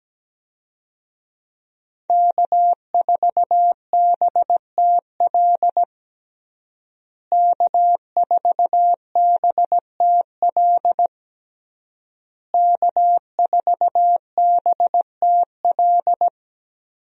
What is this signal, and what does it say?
Morse code, heard as a keyed tone, about 17 wpm, 700 Hz. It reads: K4BTL K4BTL K4BTL
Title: USA Call Signs - 5 Characters - Repeated 3x - 17wpm